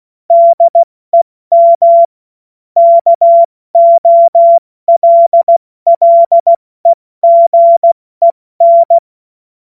Morse code, heard as a keyed tone, about 16 wpm, 680 Hz. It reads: DEM KOLLEGEN
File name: Morsekurs / Witze / Witz 10. Januar 2023 70 F80